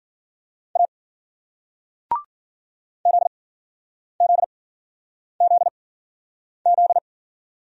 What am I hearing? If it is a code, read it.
Morse code, heard as a keyed tone, about 53 wpm, spaced as for 9 wpm, 700 Hz. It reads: A ZZZZ